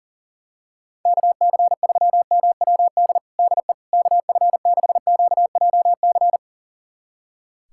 Morse code, heard as a keyed tone, about 40 wpm, 700 Hz. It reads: KC3MWD DE KF6QJC
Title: USA Callsign DE Callsign 40wpm